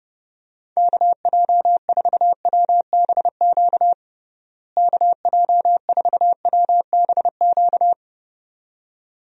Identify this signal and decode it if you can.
Morse code, heard as a keyed tone, about 30 wpm, 700 Hz. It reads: KJ4WBQ KJ4WBQ
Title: USA Call Signs - 6 Characters - Repeated 2x - 30wpm